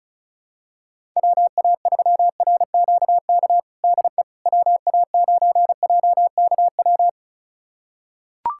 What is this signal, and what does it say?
Morse code, heard as a keyed tone, about 36 wpm, 700 Hz. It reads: WA3RQK DE WA9JKW
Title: USA Callsign DE Callsign - 6 Characters - 35wpm